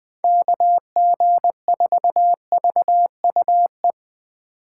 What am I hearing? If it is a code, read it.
Morse code, heard as a keyed tone, about 24 wpm, 700 Hz. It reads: KG4VUE
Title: Speed Racing - USA 2x3 Call Signs 20wpm